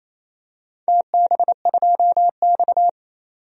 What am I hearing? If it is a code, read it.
Morse code, heard as a keyed tone, about 27 wpm, 700 Hz. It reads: TB2X
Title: DXCC-100 Call Signs - 4 Characters - 28wpm